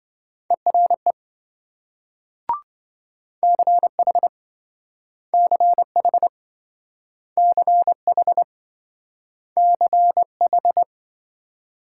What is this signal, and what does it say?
Morse code, heard as a keyed tone, about 26 wpm, 700 Hz. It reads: ERE CH CH CH CH